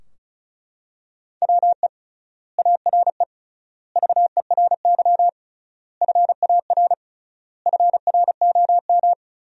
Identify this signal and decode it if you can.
Morse code, heard as a keyed tone, about 36 wpm, 700 Hz. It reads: WE ARE VERY FAR FROM
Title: 3x Word-Spacing - Sentences from Top 300 Words - 35wpm